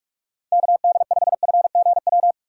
Morse code, heard as a keyed tone, about 46 wpm, 700 Hz. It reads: KD5FGW
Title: USA Call Signs - 6 Characters - 4x Character Spacing - 45wpm